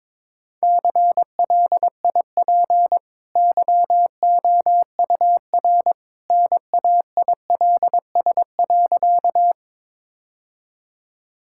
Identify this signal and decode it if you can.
Morse code, heard as a keyed tone, about 22 wpm, 700 Hz. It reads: CLIP YOUR NAILS.
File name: practice / 22wpm / Easy Sentences with Hints - 22wpm